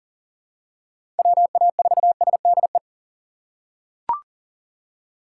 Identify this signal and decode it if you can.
Morse code, heard as a keyed tone, about 40 wpm, 700 Hz. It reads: WA4SDE
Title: USA Call Signs - 6 Characters - 2x Character Spacing - 40wpm